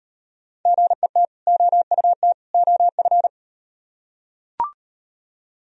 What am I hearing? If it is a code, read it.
Morse code, heard as a keyed tone, about 36 wpm, 700 Hz. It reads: GET OUT OF